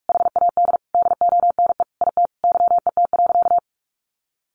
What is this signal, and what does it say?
Morse code, heard as a keyed tone, about 44 wpm, 700 Hz. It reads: HAD DONE IT YET.
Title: Sentences from Top 1100 Words 45wpm